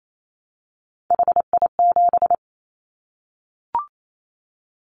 Morse code, heard as a keyed tone, about 28 wpm, 700 Hz. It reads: HI7
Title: Callsign Prefixes for Top 100 DXCC - 28wpm